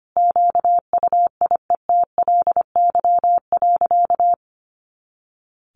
Morse code, heard as a keyed tone, about 25 wpm, 700 Hz. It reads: QUIETLY.
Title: A Sound Of Thunder 25wpm